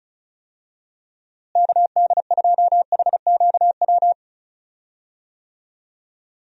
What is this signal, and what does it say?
Morse code, heard as a keyed tone, about 34 wpm, 700 Hz. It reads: KD2HQW
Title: USA Call Signs - 6 Characters - Repeated 2x - 35wpm